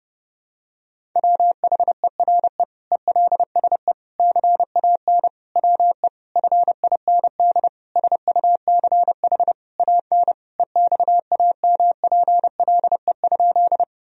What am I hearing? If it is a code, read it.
Morse code, heard as a keyed tone, about 30 wpm, 700 Hz. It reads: WHERE ELSE CAN WE FIND SUCH AN EXAMPLE?
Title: Sentences from Top 1000 Words - Review Part 1 - 30wpm